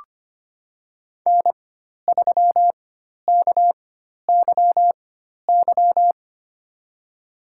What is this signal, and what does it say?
Morse code, heard as a keyed tone, about 25 wpm, spaced as for 12 wpm, 700 Hz. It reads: N3KYY
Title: USA Call Signs - 4x Character Spacing - 25wpm